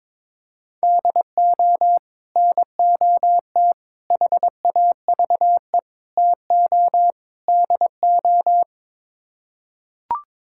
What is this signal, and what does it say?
Morse code, heard as a keyed tone, about 21 wpm, 700 Hz. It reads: DO NOT HAVE TO DO
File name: practice / 22wpm / Build A Phrase - 5-Word Phrases - 22wpm